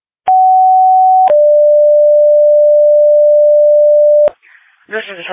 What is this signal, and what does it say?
Two-tone page, heard as a one-second tone, then a three-second tone, about 750 then 590 Hz. It is heard from 0.3 to 4.3 s.